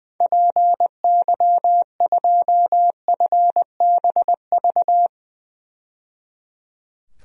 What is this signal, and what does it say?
Morse code, heard as a keyed tone, about 20 wpm, 700 Hz. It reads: PY2FBV
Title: DXCC-100 Call Signs - 6 Characters - 20wpm